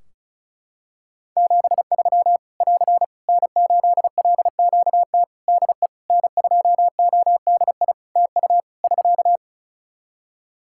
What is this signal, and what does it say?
Morse code, heard as a keyed tone, about 34 wpm, 700 Hz. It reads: 73 <AR> N8LQT DE N2ODI TU <SK>
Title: K1USN SST QSO Practice 35wpm